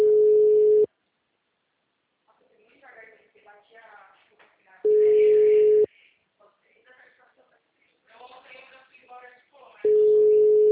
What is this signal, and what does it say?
Ringback tone (Europe, 425 Hz), with speech.